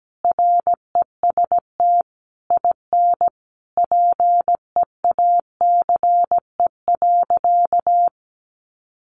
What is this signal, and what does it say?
Morse code, heard as a keyed tone, about 17 wpm, 700 Hz. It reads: REST IN PEACE.
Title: Sentences from Top 1500 Words 17wpm